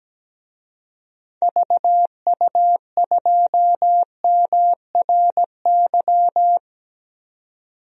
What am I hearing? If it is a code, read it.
Morse code, heard as a keyed tone, about 17 wpm, 700 Hz. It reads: VU2MRY
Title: DXCC-100 Call Signs - 6 Characters - 17wpm